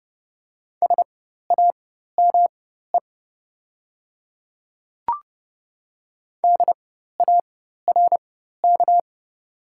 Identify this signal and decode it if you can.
Morse code, heard as a keyed tone, about 30 wpm, spaced as for 14 wpm, 690 Hz. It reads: SAME DARK
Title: 4x Character Spacing - Top 500 Words - 30wpm